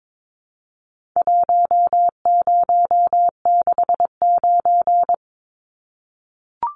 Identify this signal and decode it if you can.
Morse code, heard as a keyed tone, about 22 wpm, 700 Hz. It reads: 1069
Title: Numbers - Contest Style - 4-Digit 22wpm